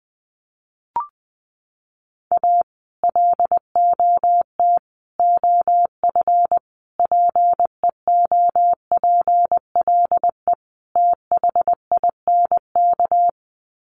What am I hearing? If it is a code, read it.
Morse code, heard as a keyed tone, about 20 wpm, 700 Hz. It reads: A LOT OF PEOPLE THINK
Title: Sets of 5 Words Encore 20wpm